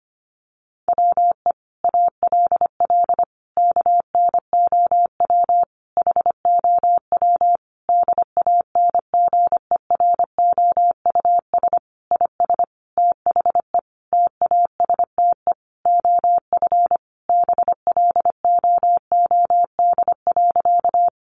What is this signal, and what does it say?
Morse code, heard as a keyed tone, about 25 wpm, 700 Hz. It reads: WE ALL KNOW HOW DANGEROUS IS THE TASTE OF BLOOD.
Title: Sentences from Top 1200 Words 25wpm